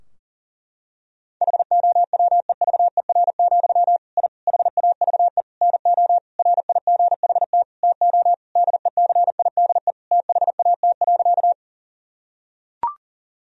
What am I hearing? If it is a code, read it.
Morse code, heard as a keyed tone, about 40 wpm, 700 Hz. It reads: HOWEVER, I HAVE NO RIGHT TO DECIDE THAT.
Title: Sentences from Top 1000 Words - Review Part 3 - 40wpm